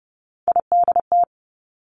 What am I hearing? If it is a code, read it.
Morse code, heard as a keyed tone, about 30 wpm, 700 Hz. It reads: IDT
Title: Three Challenging Letter-Number Combinations 30wpm